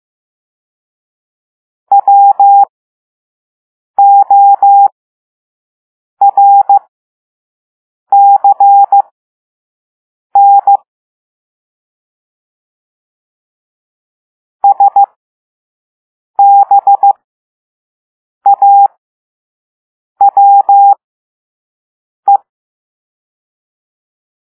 Morse code, heard as a keyed tone, about 15 wpm, spaced as for 5 wpm, 800 Hz. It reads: WORCN SBAWE